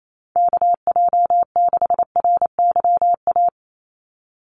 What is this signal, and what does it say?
Morse code, heard as a keyed tone, about 28 wpm, 700 Hz. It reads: KJ6RYA
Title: USA Call Signs 28wpm